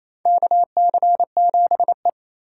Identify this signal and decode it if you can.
Morse code, heard as a keyed tone, about 28 wpm, 700 Hz. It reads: KC7E
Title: USA Call Signs - 4 Characters - 3x Character Spacing - 28wpm